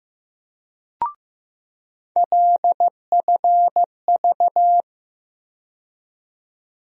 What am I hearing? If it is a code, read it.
Morse code, heard as a keyed tone, about 15 wpm, 700 Hz. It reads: LFV
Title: USA Call Sign Postfixes - 3 Letters 15wpm